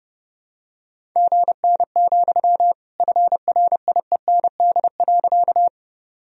Morse code, heard as a keyed tone, about 30 wpm, 700 Hz. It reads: GN, FRIEND.
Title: Sentences from Top 300 Words and Abbreviations 30wpm